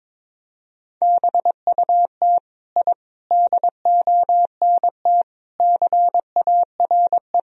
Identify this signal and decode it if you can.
Morse code, heard as a keyed tone, about 21 wpm, 700 Hz. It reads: BUT I DONT CARE
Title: Sentences from Top 1000 Words - Review Part 1 - 22wpm